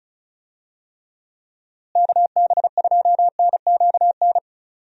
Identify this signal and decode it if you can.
Morse code, heard as a keyed tone, about 34 wpm, 700 Hz. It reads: KB2NQN
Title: USA Call Signs - 6 Characters - Repeated 2x - 35wpm